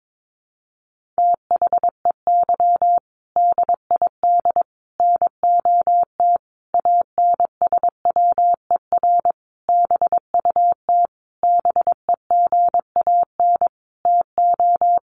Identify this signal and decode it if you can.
Morse code, heard as a keyed tone, about 22 wpm, 700 Hz. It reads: THEY DID NOT ANSWER BUT BEGAN TO
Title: Sentences from Top 600 Words 22wpm